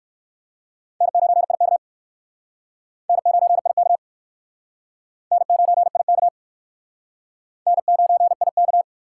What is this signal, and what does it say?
Morse code, heard as a keyed tone, about 55 wpm, 700 Hz. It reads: N9IK N9IK N9IK N9IK
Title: Speed Racing - USA Call Signs 45wpm